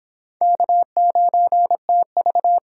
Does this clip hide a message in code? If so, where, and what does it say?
Yes: Morse code from 0.4 to 2.6 s, heard as a keyed tone, about 30 wpm, 700 Hz. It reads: K9TV